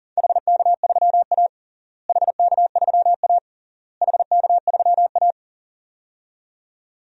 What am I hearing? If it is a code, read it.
Morse code, heard as a keyed tone, about 40 wpm, 700 Hz. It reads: HK3A HK3A HK3A
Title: DXCC-100 Call Signs - 4 Characters - Repeated 3x - 40wpm